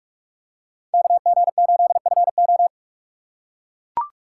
Morse code, heard as a keyed tone, about 44 wpm, 700 Hz. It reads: KG8FO